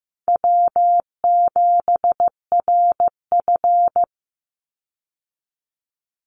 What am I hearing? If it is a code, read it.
Morse code, heard as a keyed tone, about 15 wpm, 700 Hz. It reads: W7RF
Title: Easy Call Signs Repeated 2x - 15wpm